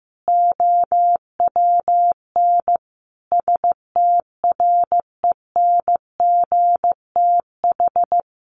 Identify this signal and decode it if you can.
Morse code, heard as a keyed tone, about 14 wpm, 700 Hz. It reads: OWN STRENGTH
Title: Sentences from Top 1300 Words 15wpm